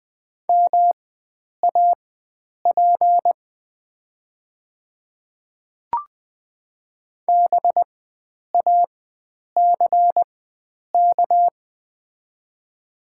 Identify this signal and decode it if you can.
Morse code, heard as a keyed tone, about 20 wpm, spaced as for 9 wpm, 700 Hz. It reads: MAP BACK
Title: 4x Character Spacing - Top 500 Words - 20wpm